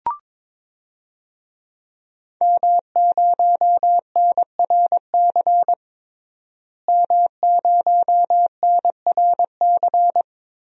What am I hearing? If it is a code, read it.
Morse code, heard as a keyed tone, about 22 wpm, 700 Hz. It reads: M0NRC M0NRC
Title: DXCC-100 Call Signs - 5 Characters - Repeated 3x - 22wpm